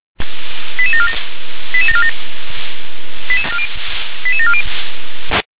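Five-tone selective call (ZVEI), heard as quick runs of five tones, tones of about 70 ms. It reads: 88747 88747 88740
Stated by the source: ZVEI five-tone selective call received on an SDR.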